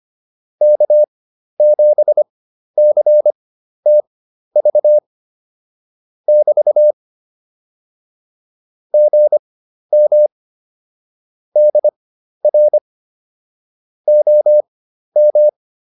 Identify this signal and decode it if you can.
Morse code, heard as a keyed tone, about 25 wpm, spaced as for 12 wpm, 600 Hz. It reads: K7CTV <BT> GM DR OM
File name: QSO 204 12